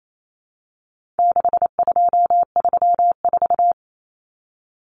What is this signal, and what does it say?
Morse code, heard as a keyed tone, about 28 wpm, 700 Hz. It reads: 6234
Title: Common Cut Numbers - 4-Digit 28wpm